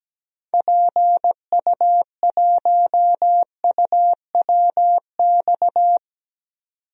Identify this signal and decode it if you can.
Morse code, heard as a keyed tone, about 17 wpm, 700 Hz. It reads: PU1UWX
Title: DXCC-100 Call Signs - 6 Characters - 17wpm